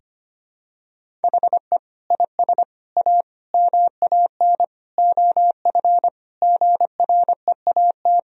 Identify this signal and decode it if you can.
Morse code, heard as a keyed tone, about 25 wpm, 700 Hz. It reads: HE IS A MAN OF GREAT
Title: Sentences from Top 1000 Words - Review Part 3 - 25wpm